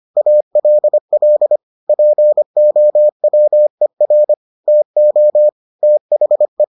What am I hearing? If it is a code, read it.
Morse code, heard as a keyed tone, about 25 wpm, 600 Hz. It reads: ALL POWER TO THE